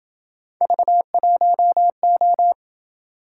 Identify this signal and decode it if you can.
Morse code, heard as a keyed tone, about 27 wpm, 700 Hz. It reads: V1O